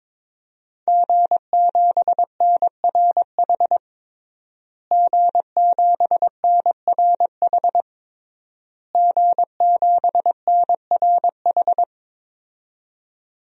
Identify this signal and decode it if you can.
Morse code, heard as a keyed tone, about 22 wpm, 700 Hz. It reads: G7NRH G7NRH G7NRH
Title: DXCC-100 Call Signs - 5 Characters - Repeated 3x - 22wpm